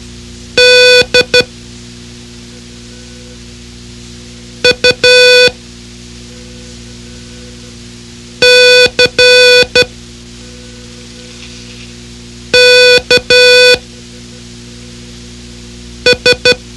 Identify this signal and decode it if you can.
Morse code, heard as a keyed tone, about 12 wpm, spaced as for 3 wpm, 500 Hz. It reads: DUCKS
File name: ham radio / CodeQuick / Code Quick 04